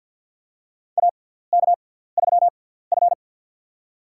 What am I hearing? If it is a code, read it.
Morse code, heard as a keyed tone, about 47 wpm, spaced as for 17 wpm, 700 Hz. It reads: AK3F
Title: USA Call Signs - 4 Characters - 6x Character Spacing - 50wpm